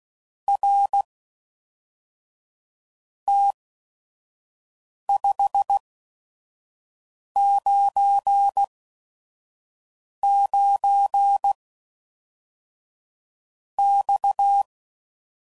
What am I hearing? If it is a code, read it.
Morse code, heard as a keyed tone, about 16 wpm, spaced as for 4 wpm, 800 Hz. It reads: RT 599 X
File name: CW Code Groups @5WPM / QSOs / QSO WA7QOX 1